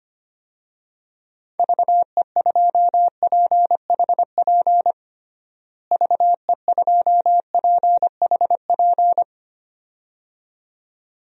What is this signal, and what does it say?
Morse code, heard as a keyed tone, about 25 wpm, 700 Hz. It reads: VE2PHP VE2PHP
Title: DXCC-100 Call Signs - 6 Characters - Repeated 2x - 25wpm